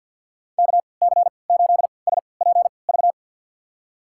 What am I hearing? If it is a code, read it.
Morse code, heard as a keyed tone, about 47 wpm, spaced as for 28 wpm, 700 Hz. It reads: KC8SPV
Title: USA Call Signs - 3x Character Spacing - 50wpm